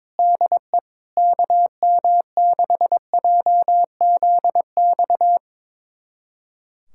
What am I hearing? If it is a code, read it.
Morse code, heard as a keyed tone, about 21 wpm, 700 Hz. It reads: DE KM6JZX
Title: USA Callsign DE Callsign - 6 Characters - 22wpm